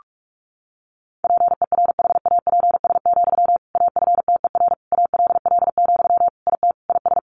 Morse code, heard as a keyed tone, about 44 wpm, 700 Hz. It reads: PERHAPS, AFTER ALL, IT IS
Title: Sentences from Top 500 Words 45wpm